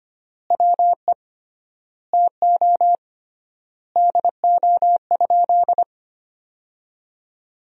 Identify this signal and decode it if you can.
Morse code, heard as a keyed tone, about 25 wpm, 700 Hz. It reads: WE TO DO?
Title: 3x Word-Spacing - Sentences from Top 100 Words - 25wpm